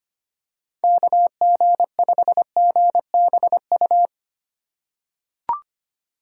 Morse code, heard as a keyed tone, about 25 wpm, 690 Hz. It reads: KG5GBU
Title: USA Call Signs - 6 Characters - 2x Character Spacing - 25wpm